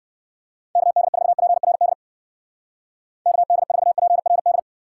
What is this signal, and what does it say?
Morse code, heard as a keyed tone, about 56 wpm, 700 Hz. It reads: KD4PRD KD4PRD
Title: Speed Racing - USA 2x3 Call Signs 45wpm